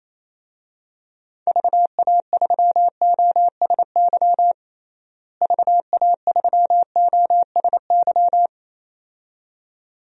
Morse code, heard as a keyed tone, about 28 wpm, 700 Hz. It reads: VA3OSY VA3OSY
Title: DXCC-100 Call Signs - 6 Characters - Repeated 2x - 28wpm